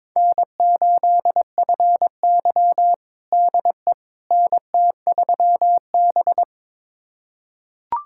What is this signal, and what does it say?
Morse code, heard as a keyed tone, about 23 wpm, 700 Hz. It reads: N8FY DE NT3B
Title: USA Callsign DE Callsign - 4 Characters - 22wpm